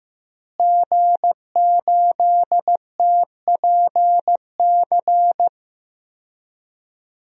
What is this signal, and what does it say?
Morse code, heard as a keyed tone, about 15 wpm, 700 Hz. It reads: G8TPC
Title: DXCC-100 Call Signs - 5 Characters - Repeated 3x - 15wpm